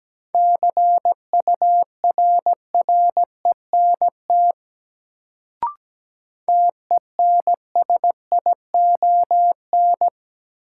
Morse code, heard as a keyed tone, about 17 wpm, 700 Hz. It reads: CURRENT TENSION
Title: Sentences from Top 1600 Words - New Word Drill Alt - 17wpm